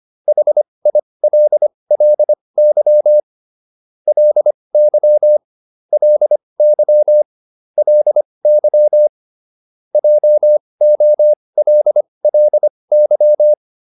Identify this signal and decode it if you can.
Morse code, heard as a keyed tone, about 25 wpm, spaced as for 20 wpm, 600 Hz. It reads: HILLY LY LY LY JOLLY